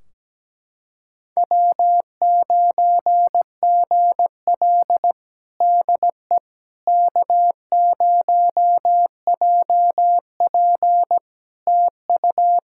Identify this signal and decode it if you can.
Morse code, heard as a keyed tone, about 17 wpm, 700 Hz. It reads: W9GL DE K0JP TU
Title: Practice QSO - Exchange 4 - Ending - 17wpm